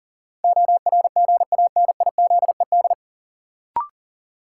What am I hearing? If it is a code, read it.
Morse code, heard as a keyed tone, about 39 wpm, 700 Hz. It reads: ORGANIZED
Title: Sentences from Top 1300 Words - New Word Drill - 40wpm